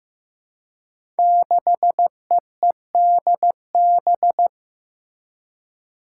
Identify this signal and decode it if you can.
Morse code, heard as a keyed tone, about 16 wpm, 700 Hz. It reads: 6EEDB
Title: Five Challenging Letter-Number Combinations 15wpm